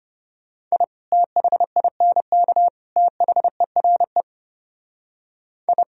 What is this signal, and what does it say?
Morse code, heard as a keyed tone, about 31 wpm, 700 Hz. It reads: I THINK THERE I
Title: Speed-Racing - Sets of 3 Words 25wpm